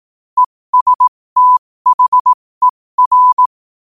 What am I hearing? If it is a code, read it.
Morse code, heard as a keyed tone, about 18 wpm, spaced as for 15 wpm, 1000 Hz. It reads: ESTHER
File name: qotdfiles / 15wpm